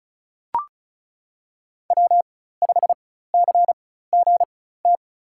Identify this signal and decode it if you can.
Morse code, heard as a keyed tone, about 34 wpm, spaced as for 16 wpm, 700 Hz. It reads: W5CGT
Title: USA Call Signs - 5 Characters - 4x Character Spacing - 35wpm